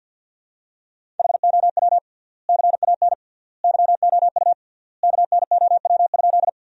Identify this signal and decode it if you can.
Morse code, heard as a keyed tone, about 50 wpm, 700 Hz. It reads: HOW CAN YOU KNOW?
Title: 3x Word-Spacing - Sentences from Top 100 Words - 50wpm